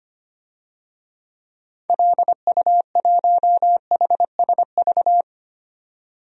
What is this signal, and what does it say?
Morse code, heard as a keyed tone, about 25 wpm, 700 Hz. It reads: LU1HSV
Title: DXCC-100 Call Signs - 6 Characters - 25wpm